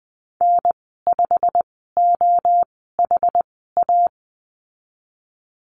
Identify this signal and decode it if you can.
Morse code, heard as a keyed tone, about 20 wpm, spaced as for 15 wpm, 700 Hz. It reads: N5OHA
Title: 147 Lesson 24g - Letter y - Call Signs 20wpm